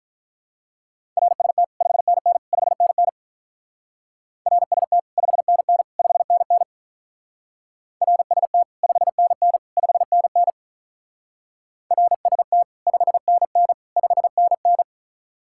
Morse code, heard as a keyed tone, about 46 wpm, 700 Hz. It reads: RST 5NN 5NN RST 5NN 5NN RST 5NN 5NN RST 5NN 5NN